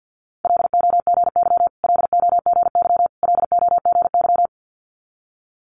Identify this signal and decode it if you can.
Morse code, heard as a keyed tone, about 50 wpm, 690 Hz. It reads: LOGY LOGY LOGY
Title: Multi-letter Phonemes Repeated 3x 50wpm